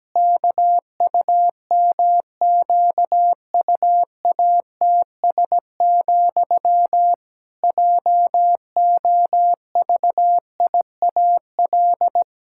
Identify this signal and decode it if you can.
Morse code, heard as a keyed tone, about 17 wpm, 700 Hz. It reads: KUMQUATS, JOVIAL
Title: Pangram Challenge v2 - 17wpm